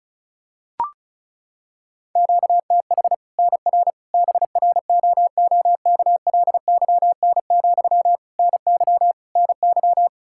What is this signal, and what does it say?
Morse code, heard as a keyed tone, about 36 wpm, 700 Hz. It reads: QTH NR BROOKLYN, NY NY
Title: Practice QSO - Exchange 1 - Beginning - 35wpm